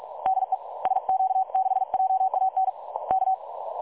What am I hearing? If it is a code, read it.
Morse code, heard as a keyed tone, about 36 wpm, 760 Hz. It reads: DE SQ6JAN K
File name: dlazaawansowanych / nagraniazpasma / websdr recording 3565.0kHz